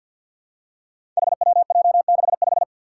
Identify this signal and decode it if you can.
Morse code, heard as a keyed tone, about 51 wpm, 700 Hz. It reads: HWJ65
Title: Five Challenging Letter-Number Combinations 50wpm